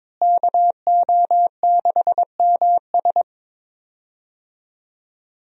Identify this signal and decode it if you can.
Morse code, heard as a keyed tone, about 21 wpm, 700 Hz. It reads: KO6MS
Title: USA Call Signs - 5 Characters - Repeated 2x - 22wpm